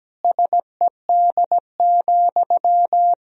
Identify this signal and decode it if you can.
Morse code, heard as a keyed tone, about 17 wpm, 700 Hz. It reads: SED,
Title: Sentences from Top 300 Words and Abbreviations 17wpm